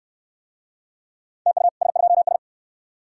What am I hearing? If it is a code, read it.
Morse code, heard as a keyed tone, about 61 wpm, 700 Hz. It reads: TU S2S